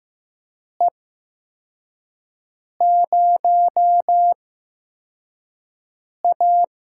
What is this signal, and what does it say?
Morse code, heard as a keyed tone, about 15 wpm, spaced as for 4 wpm, 700 Hz. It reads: E0A
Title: USA Call Signs - 8x Character Spacing - 15wpm